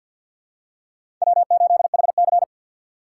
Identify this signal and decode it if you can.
Morse code, heard as a keyed tone, about 52 wpm, 700 Hz. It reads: W8HC
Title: USA Call Signs - 4 Characters - 6x Character Spacing - 50wpm